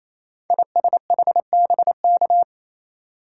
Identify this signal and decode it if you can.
Morse code, heard as a keyed tone, about 28 wpm, 700 Hz. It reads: ISHBK
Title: Code Groups - 5 Letters - 28wpm